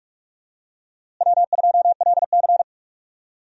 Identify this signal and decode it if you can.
Morse code, heard as a keyed tone, about 46 wpm, 700 Hz. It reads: W2LC
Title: USA Call Signs - 4 Characters - Repeated 2x - 45wpm